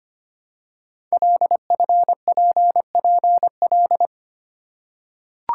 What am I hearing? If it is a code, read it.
Morse code, heard as a keyed tone, about 25 wpm, 700 Hz. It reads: LFPPL
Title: Code Groups - 5 Letters - 25wpm